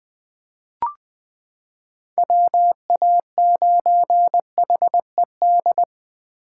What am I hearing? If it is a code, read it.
Morse code, heard as a keyed tone, about 20 wpm, 700 Hz. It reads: WA9HED
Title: 2023 SOTA Chaser Call Signs - 20wpm